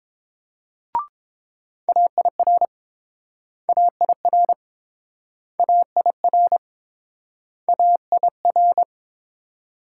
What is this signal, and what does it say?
Morse code, heard as a keyed tone, about 30 wpm, 700 Hz. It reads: AIR AIR AIR AIR